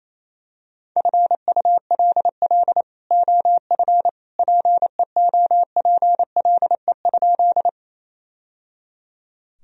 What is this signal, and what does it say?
Morse code, heard as a keyed tone, about 28 wpm, 700 Hz. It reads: FULL OF PEOPLE?
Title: Sentences from Top 1000 Words - Review Part 3 - 28wpm